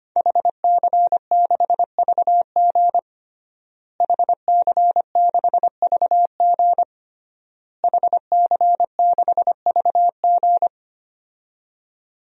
Morse code, heard as a keyed tone, about 25 wpm, 700 Hz. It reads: HC6VG HC6VG HC6VG
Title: DXCC-100 Call Signs - 5 Characters - Repeated 3x - 25wpm